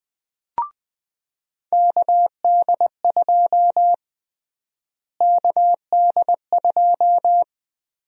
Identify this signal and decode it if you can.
Morse code, heard as a keyed tone, about 20 wpm, 700 Hz. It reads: KD2 KD2
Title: USA Call Sign Prefixes - Repeated 2x - 20wpm